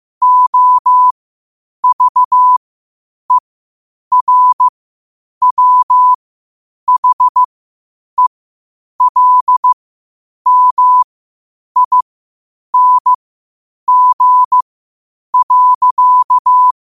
Morse code, heard as a keyed tone, about 15 wpm, spaced as for 8 wpm, 1000 Hz. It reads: OVERWHELMING.
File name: 8wpm